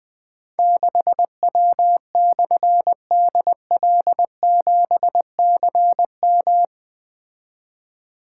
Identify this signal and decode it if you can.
Morse code, heard as a keyed tone, about 20 wpm, 700 Hz. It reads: 6W/DL7CM
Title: Difficult Call Signs 20wpm